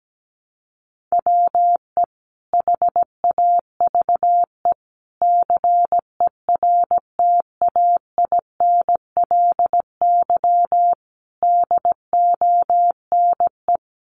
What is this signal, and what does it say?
Morse code, heard as a keyed tone, about 17 wpm, 700 Hz. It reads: WE HAVE CERTAINLY DONE
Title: Sentences from Top 900 Words 17wpm